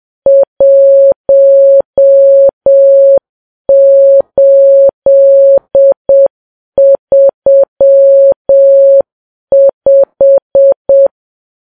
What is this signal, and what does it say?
Morse code, heard as a keyed tone, about 7 wpm, 550 Hz. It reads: 1835